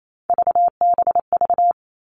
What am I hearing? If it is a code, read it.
Morse code, heard as a keyed tone, about 28 wpm, 700 Hz. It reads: VBV
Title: All Cut Numbers - 3-Digit 28wpm